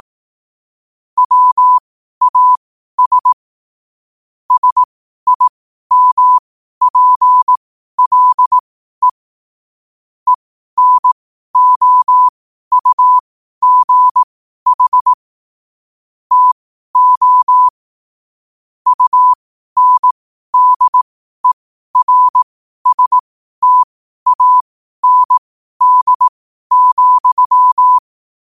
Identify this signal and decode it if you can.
Morse code, heard as a keyed tone, about 18 wpm, spaced as for 12 wpm, 1000 Hz. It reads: WAS SIMPLE ENOUGH TO UNDERSTAND,
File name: qotdfiles / 12wpm